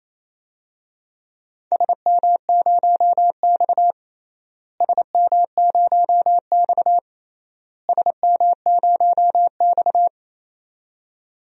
Morse code, heard as a keyed tone, about 27 wpm, 700 Hz. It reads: SM0X SM0X SM0X
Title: DXCC-100 Call Signs - 4 Characters - Repeated 3x - 28wpm